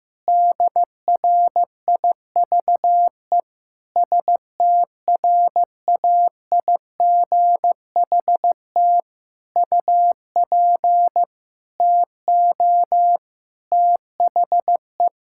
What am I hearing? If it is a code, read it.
Morse code, heard as a keyed tone, about 15 wpm, 700 Hz. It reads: DRIVE STRAIGHT UP TO THE
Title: Sentences from Top 1000 Words - Review Part 6 - 15wpm